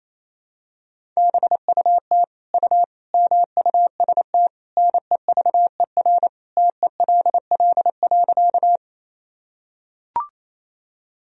Morse code, heard as a keyed tone, about 28 wpm, 700 Hz. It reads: BUT U MUST NEVER TELL.
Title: Sentences from Top 300 Words and Abbreviations 28wpm